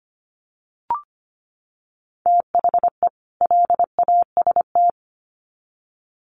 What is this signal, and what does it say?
Morse code, heard as a keyed tone, about 25 wpm, 700 Hz. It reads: THE LAST
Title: Sets of 2 Words 25wpm